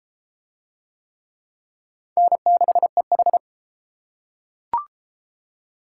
Morse code, heard as a keyed tone, about 29 wpm, 700 Hz. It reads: N6EH